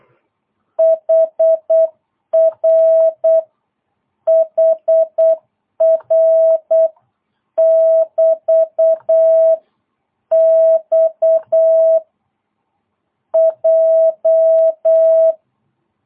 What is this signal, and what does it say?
Morse code, heard as a keyed tone, about 8 wpm, 660 Hz. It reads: HR HR <BT> X J
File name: jh8cbh / 2022cw